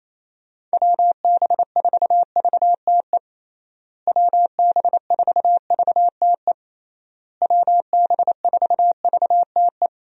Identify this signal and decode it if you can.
Morse code, heard as a keyed tone, about 28 wpm, 700 Hz. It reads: WB4VTE WB4VTE WB4VTE
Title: USA Call Signs - 6 Characters - Repeated 3x - 28wpm